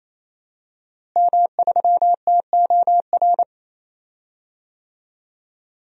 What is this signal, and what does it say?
Morse code, heard as a keyed tone, about 28 wpm, 700 Hz. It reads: M3TOR
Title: DXCC-100 Call Signs - 5 Characters - Repeated 3x - 28wpm